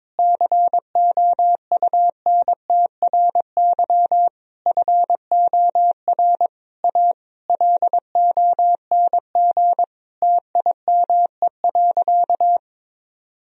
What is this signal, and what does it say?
Morse code, heard as a keyed tone, about 22 wpm, 700 Hz. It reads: COUNTRY FOR A LONG TIME.
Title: Sentences from Top 1000 Words - Review Part 1 - 22wpm